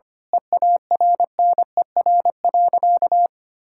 Morse code, heard as a keyed tone, about 25 wpm, 700 Hz. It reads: EARNER.
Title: Pangram Challenge v2 - 25wpm